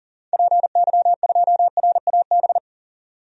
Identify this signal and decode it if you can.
Morse code, heard as a keyed tone, about 40 wpm, 700 Hz. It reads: PY2RAB
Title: DXCC-100 Call Signs - 6 Characters - 40wpm